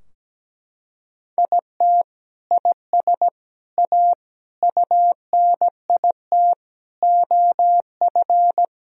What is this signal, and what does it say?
Morse code, heard as a keyed tone, about 17 wpm, 700 Hz. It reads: IT IS A UNIT OF
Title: Sentences from Top 1000 Words - Review Part 2 - 17wpm